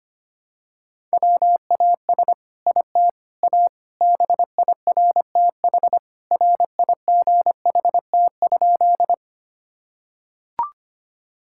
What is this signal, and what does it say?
Morse code, heard as a keyed tone, about 25 wpm, 700 Hz. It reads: WAS IT A BIRTH RIGHT?
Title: Sentences from Top 1000 Words - Review Part 5 - 25wpm